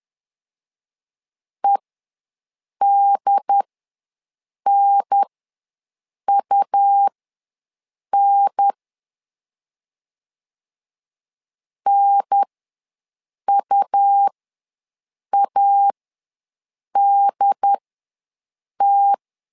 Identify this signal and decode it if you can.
Morse code, heard as a keyed tone, about 10 wpm, spaced as for 6 wpm, 800 Hz. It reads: EDNUN NUADT
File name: 2 UHDN ws 10wpm